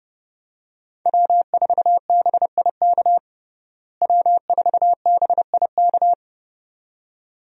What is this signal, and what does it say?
Morse code, heard as a keyed tone, about 32 wpm, 700 Hz. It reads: W4BIK W4BIK
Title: USA Call Signs - 5 Characters - Repeated 2x - 30wpm